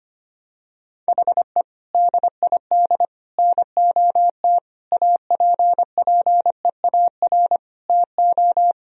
Morse code, heard as a keyed tone, about 25 wpm, 700 Hz. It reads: HE DID NOT APPEAR TO
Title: Sentences from Top 1000 Words - Review Part 2 - 25wpm